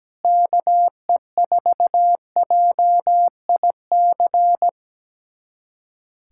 Morse code, heard as a keyed tone, about 17 wpm, 700 Hz. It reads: KE4JIC
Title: USA Callsign DE Callsign - 6 Characters - 17wpm